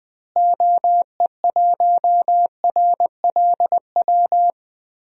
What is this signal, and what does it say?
Morse code, heard as a keyed tone, about 20 wpm, 700 Hz. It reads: OE1RLW
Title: DXCC-100 Call Signs - 6 Characters - Repeated 3x - 20wpm